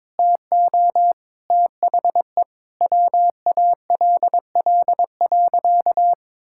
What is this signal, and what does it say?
Morse code, heard as a keyed tone, about 22 wpm, 700 Hz. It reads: TO THE WALL.
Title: Sentences from Top 1000 Words - Review Part 6 - 22wpm